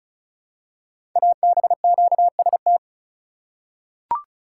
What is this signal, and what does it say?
Morse code, heard as a keyed tone, about 36 wpm, 700 Hz. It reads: ABQST
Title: Code Groups - 5 Letters - 35wpm